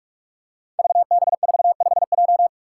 Morse code, heard as a keyed tone, about 44 wpm, 700 Hz. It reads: VB45J
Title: Code Groups - 5 Letter:Numbers - 45wpm